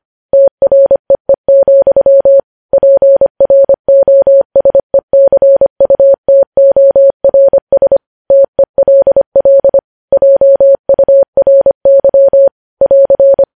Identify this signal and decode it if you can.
Morse code, heard as a keyed tone, about 25 wpm, 550 Hz. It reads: TREE, PROSECUTORS TELL JURY <AR>
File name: podcast / News headlines-1x-25x25